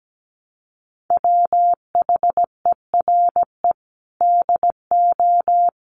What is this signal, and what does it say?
Morse code, heard as a keyed tone, about 17 wpm, 700 Hz. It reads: WHERE DO
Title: Sentences from Top 200 Words 17wpm